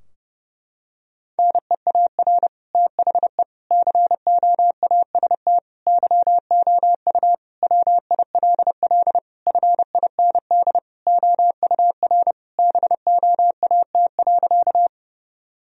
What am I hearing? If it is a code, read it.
Morse code, heard as a keyed tone, about 32 wpm, 700 Hz. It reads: NEAR THE COAST YOU WILL FIND OUR BOAT.
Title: Sentences from Top 1000 Words - Review Part 4 - 30wpm